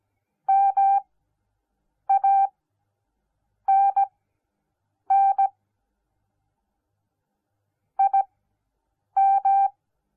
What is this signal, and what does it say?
Morse code, heard as a keyed tone, about 17 wpm, spaced as for 7 wpm, 790 Hz. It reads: MANN IM